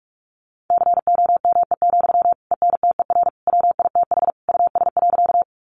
Morse code, heard as a keyed tone, about 44 wpm, 700 Hz. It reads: COME, ENTER WITH US.
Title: Sentences from Top 1300 Words 45wpm